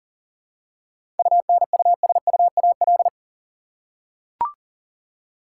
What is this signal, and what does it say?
Morse code, heard as a keyed tone, about 40 wpm, 700 Hz. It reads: UNUSUAL
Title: Seven Letter Words - 40wpm